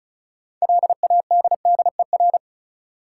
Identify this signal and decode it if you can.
Morse code, heard as a keyed tone, about 36 wpm, 700 Hz. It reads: LADDER
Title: Six Letter Words - 35wpm